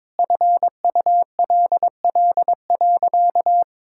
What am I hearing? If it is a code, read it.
Morse code, heard as a keyed tone, about 22 wpm, 700 Hz. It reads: FULL.
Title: Sentences from Top 1600 Words 22wpm